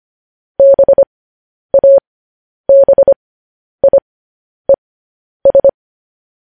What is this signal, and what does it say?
Morse code, heard as a keyed tone, about 25 wpm, spaced as for 10 wpm, 550 Hz. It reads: BABIES
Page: news-headlines-in-morse-code-at-10-wpm